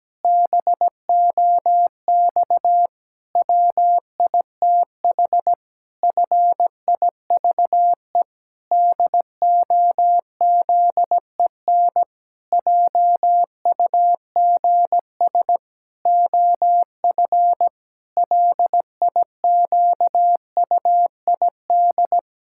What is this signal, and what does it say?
Morse code, heard as a keyed tone, about 17 wpm, 700 Hz. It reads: BOX WITH FIVE DOZEN JUGS OF LIQUID
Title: Pangram Challenge v2 - 17wpm